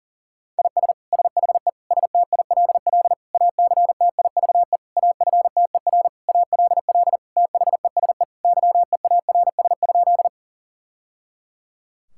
Morse code, heard as a keyed tone, about 40 wpm, 700 Hz. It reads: IS SHE STILL ACTIVE AFTER ALL THESE YEARS?
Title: Sentences from Top 1000 Words - Review Part 6 - 40wpm